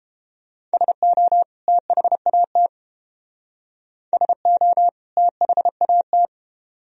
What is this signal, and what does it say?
Morse code, heard as a keyed tone, about 30 wpm, 700 Hz. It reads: SO THAT SO THAT